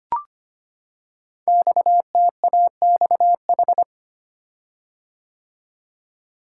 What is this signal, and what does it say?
Morse code, heard as a keyed tone, about 25 wpm, 700 Hz. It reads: XTAXH
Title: Code Groups - 5 Letters - 25wpm